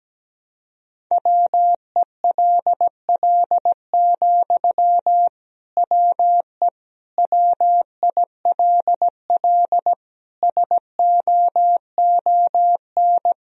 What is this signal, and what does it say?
Morse code, heard as a keyed tone, about 17 wpm, 700 Hz. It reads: WELL, WE WILL SOON
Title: Sentences from Top 1000 Words - Review Part 4 - 17wpm